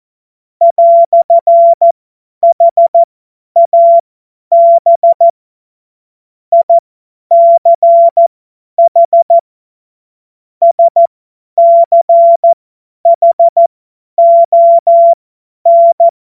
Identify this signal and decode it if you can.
Morse code, heard as a keyed tone, about 14 wpm, spaced as for 10 wpm, 680 Hz. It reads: "HAB ICH SCHON
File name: Morsekurs / Witze / Witz 23. Mai 2023 50 F70